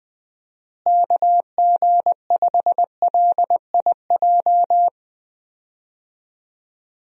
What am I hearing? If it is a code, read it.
Morse code, heard as a keyed tone, about 20 wpm, 700 Hz. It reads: KG5LIJ
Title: USA Call Signs - 6 Characters - Repeated 3x - 20wpm